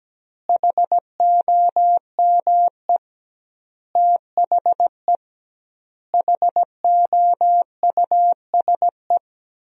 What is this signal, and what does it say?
Morse code, heard as a keyed tone, about 17 wpm, 700 Hz. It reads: HOME THE HOUSE
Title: 2x Word-Spacing - Sentences from Top 500 Words - 17wpm